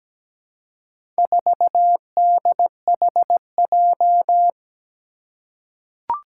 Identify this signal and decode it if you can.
Morse code, heard as a keyed tone, about 17 wpm, 700 Hz. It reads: 4DHJ
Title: Four Challenging Letter-Number Combinations 17wpm